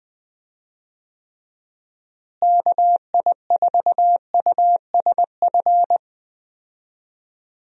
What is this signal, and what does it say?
Morse code, heard as a keyed tone, about 20 wpm, 700 Hz. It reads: KI4USF
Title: USA Call Signs - 6 Characters - Repeated 2x - 20wpm